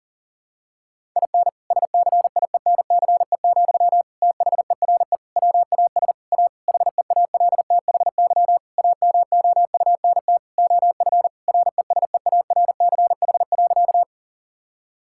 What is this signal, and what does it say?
Morse code, heard as a keyed tone, about 37 wpm, 700 Hz. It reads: IN SCIENCE, THERE WAS A HEALTHY AMOUNT OF RESEARCH.
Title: Sentences from Top 1000 Words - Review Part 5 - 40wpm